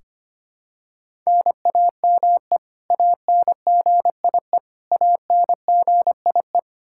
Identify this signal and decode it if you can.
Morse code, heard as a keyed tone, about 25 wpm, 700 Hz. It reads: NAME ANGIE ANGIE
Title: Practice QSO - Exchange 1 - Beginning - 25wpm